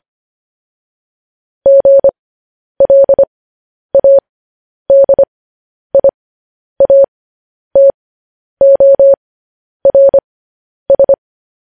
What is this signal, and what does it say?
Morse code, heard as a keyed tone, about 25 wpm, spaced as for 10 wpm, 550 Hz. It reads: GLADIATORS
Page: News Headlines in Morse Code at 10 WPM